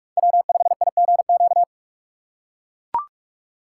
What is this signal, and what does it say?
Morse code, heard as a keyed tone, about 43 wpm, 700 Hz. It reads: W5IGQ
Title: USA Call Signs - 2x Character Spacing - 45wpm